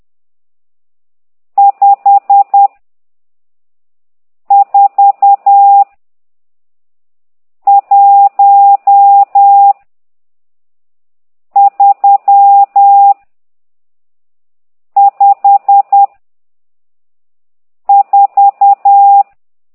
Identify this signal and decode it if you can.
Morse code, heard as a keyed tone, about 10 wpm, 800 Hz. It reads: 5 4 1 3 5 4
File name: ct2idw / exerc12